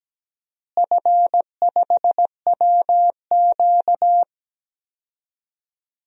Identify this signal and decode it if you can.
Morse code, heard as a keyed tone, about 17 wpm, 700 Hz. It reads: F5WQ
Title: DXCC-100 Call Signs - 4 Characters - Repeated 2x - 17wpm